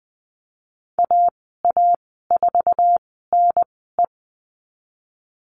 Morse code, heard as a keyed tone, about 20 wpm, spaced as for 14 wpm, 700 Hz. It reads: AA4NE
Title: 048 Lesson 9g - Number 4 - Call Signs 20wpm